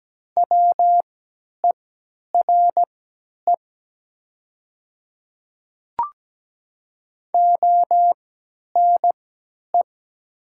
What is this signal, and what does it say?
Morse code, heard as a keyed tone, about 17 wpm, spaced as for 10 wpm, 700 Hz. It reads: WERE ONE
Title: 3x Character Spacing - Top 100 Words - 17wpm